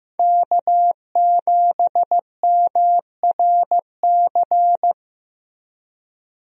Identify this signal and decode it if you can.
Morse code, heard as a keyed tone, about 15 wpm, 700 Hz. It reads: K7MRC
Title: USA Call Signs - 5 Characters - Repeated 2x - 15wpm